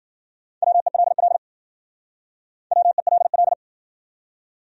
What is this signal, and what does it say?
Morse code, heard as a keyed tone, about 60 wpm, 700 Hz. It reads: WELL WELL